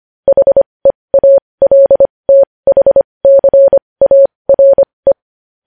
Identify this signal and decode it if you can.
Morse code, heard as a keyed tone, about 25 wpm, spaced as for 20 wpm, 550 Hz. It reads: HEALTHCARE
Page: news headlines in morse code at 20 wpm